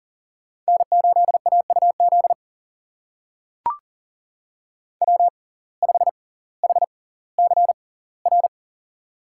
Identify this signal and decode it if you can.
Morse code, heard as a keyed tone, about 40 wpm, spaced as for 20 wpm, 700 Hz. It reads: N8AUZ W5HCR
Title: USA Call Signs - 6x Character Spacing - 40wpm